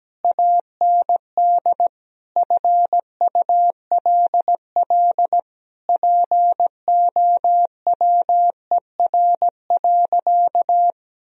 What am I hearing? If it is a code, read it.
Morse code, heard as a keyed tone, about 17 wpm, 700 Hz. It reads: AND FULL POWER.
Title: Sentences from Top 1000 Words - Review Part 3 - 17wpm